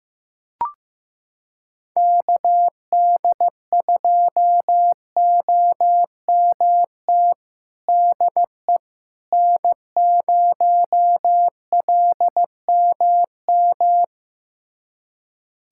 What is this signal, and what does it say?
Morse code, heard as a keyed tone, about 15 wpm, 700 Hz. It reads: KD2OMT DE N0LMM
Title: USA Callsign DE Callsign 15wpm